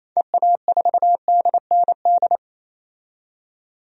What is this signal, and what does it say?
Morse code, heard as a keyed tone, about 28 wpm, 700 Hz. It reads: EA4DND
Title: DXCC-100 Call Signs - 6 Characters - 28wpm